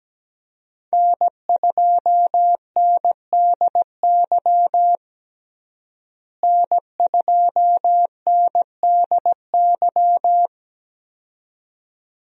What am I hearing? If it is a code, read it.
Morse code, heard as a keyed tone, about 17 wpm, 700 Hz. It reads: N2NDY N2NDY
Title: USA Call Signs - 5 Characters - Repeated 2x - 17wpm